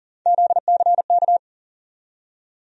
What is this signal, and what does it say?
Morse code, heard as a keyed tone, about 39 wpm, 700 Hz. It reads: ZCK
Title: USA Call Sign Postfixes - 3 Letters 40wpm